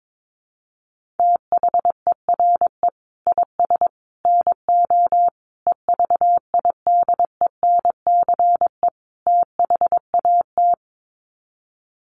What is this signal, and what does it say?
Morse code, heard as a keyed tone, about 22 wpm, 700 Hz. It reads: THERE IS NO EVIDENCE THAT
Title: Sets of 5 Words Encore 22wpm